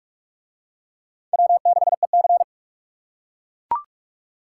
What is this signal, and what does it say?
Morse code, heard as a keyed tone, about 39 wpm, 700 Hz. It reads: W6EC